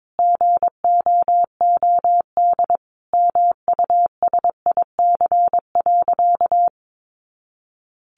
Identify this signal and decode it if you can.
Morse code, heard as a keyed tone, about 22 wpm, 700 Hz. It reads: GOOD MUSIC.
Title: Easy Phrases 22wpm